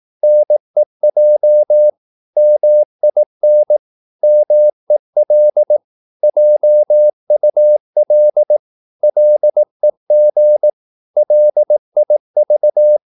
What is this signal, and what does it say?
Morse code, heard as a keyed tone, about 18 wpm, 600 Hz. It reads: NEJ MIN MEL JUL LEG LIV
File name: Korte ord DK 18wpm